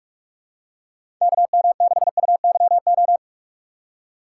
Morse code, heard as a keyed tone, about 43 wpm, 700 Hz. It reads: KM6UYO
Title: USA Call Signs - 2x Character Spacing - 45wpm